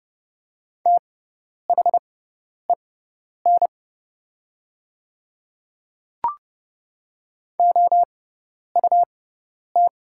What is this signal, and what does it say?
Morse code, heard as a keyed tone, about 31 wpm, spaced as for 10 wpm, 700 Hz. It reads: THEN OUT
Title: 6x Character Spacing - Top 100 Words - 30wpm